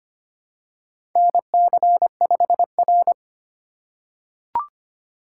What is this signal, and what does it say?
Morse code, heard as a keyed tone, about 25 wpm, 700 Hz. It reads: NC5R
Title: USA Call Signs - 4 Characters - 2x Character Spacing - 25wpm